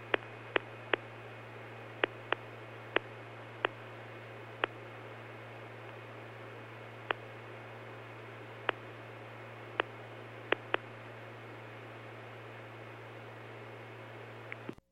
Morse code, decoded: none

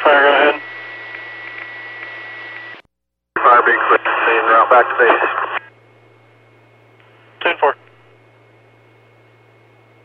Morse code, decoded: none

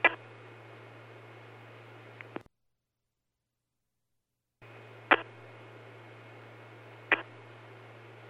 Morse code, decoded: none

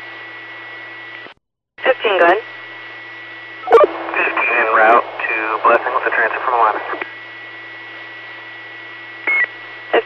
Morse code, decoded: none

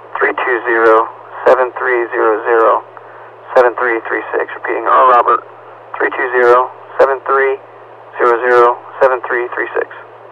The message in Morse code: none

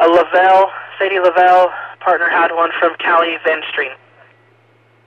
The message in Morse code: none

